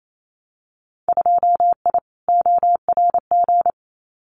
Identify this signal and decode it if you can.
Morse code, heard as a keyed tone, about 28 wpm, 700 Hz. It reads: 2I ORG
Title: Winter Field Day Exchanges Encore 28wpm